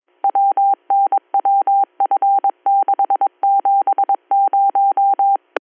Morse code, heard as a keyed tone, about 22 wpm, 790 Hz. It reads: WNWF670
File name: 310- .0-call 7515